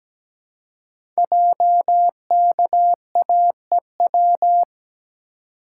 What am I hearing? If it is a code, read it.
Morse code, heard as a keyed tone, about 17 wpm, 700 Hz. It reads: JKAEW
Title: Code Groups - 5 Letters - 17wpm